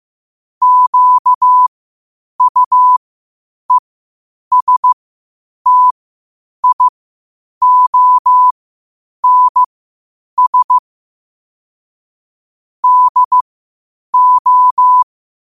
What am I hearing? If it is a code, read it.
Morse code, heard as a keyed tone, about 15 wpm, spaced as for 8 wpm, 1000 Hz. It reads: QUESTIONS DO
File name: qotdfiles / 8wpm